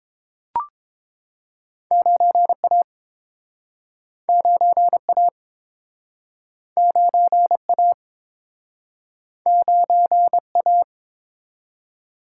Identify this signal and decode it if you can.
Morse code, heard as a keyed tone, about 31 wpm, 700 Hz. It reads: 9A 9A 9A 9A